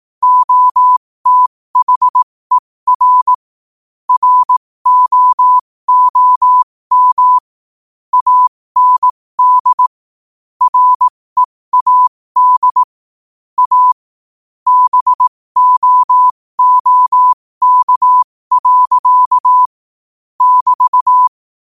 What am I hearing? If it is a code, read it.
Morse code, heard as a keyed tone, about 18 wpm, spaced as for 15 wpm, 1000 Hz. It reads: OTHER ROOM AND READ A BOOK. <BT>
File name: qotdfiles / 15wpm